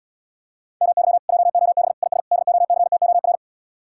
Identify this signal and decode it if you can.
Morse code, heard as a keyed tone, about 79 wpm, 700 Hz. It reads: MY JOB IS DOCTOR